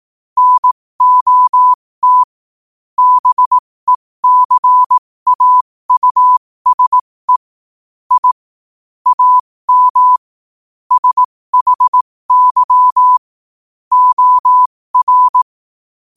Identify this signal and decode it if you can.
Morse code, heard as a keyed tone, about 18 wpm, spaced as for 15 wpm, 1000 Hz. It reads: NOT BECAUSE I AM SHY OR
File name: qotdfiles / 15wpm